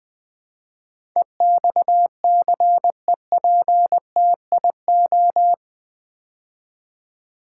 Morse code, heard as a keyed tone, about 20 wpm, 700 Hz. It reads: EXCEPTIO
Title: Build A Word - 4 Letters At A Time - 11-Letter Words - 20wpm